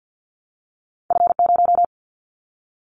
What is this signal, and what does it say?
Morse code, heard as a keyed tone, about 50 wpm, 700 Hz. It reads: F0
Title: Callsign Prefixes for Top 25 DXCC - 50wpm